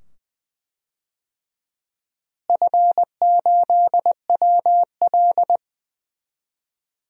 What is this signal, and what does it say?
Morse code, heard as a keyed tone, about 20 wpm, 700 Hz. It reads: F8WL
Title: DXCC-100 Call Signs - 4 Characters - Repeated 2x - 20wpm